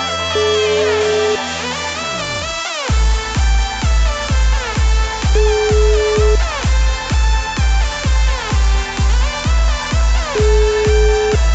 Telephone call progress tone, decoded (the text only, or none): ringback tone